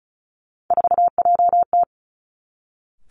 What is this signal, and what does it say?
Morse code, heard as a keyed tone, about 36 wpm, 700 Hz. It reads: 4JT
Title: Three Challenging Letter-Number Combinations 35wpm